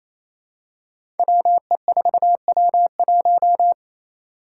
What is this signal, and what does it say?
Morse code, heard as a keyed tone, about 28 wpm, 700 Hz. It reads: WE4W1
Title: Five Challenging Letter-Number Combinations 28wpm